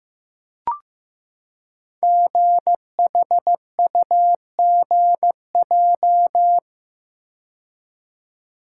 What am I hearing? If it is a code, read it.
Morse code, heard as a keyed tone, about 15 wpm, 700 Hz. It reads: GHUGJ
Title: Code Groups - 5 Letters - 15wpm